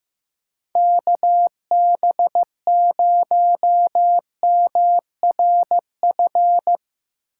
Morse code, heard as a keyed tone, about 15 wpm, 700 Hz. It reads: KB0MRF
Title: USA Call Signs - 6 Characters - 4x Character Spacing - 15wpm